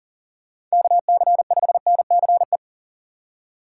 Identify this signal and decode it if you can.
Morse code, heard as a keyed tone, about 40 wpm, 700 Hz. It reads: KC5NCE
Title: USA Call Signs - 6 Characters - 4x Character Spacing - 40wpm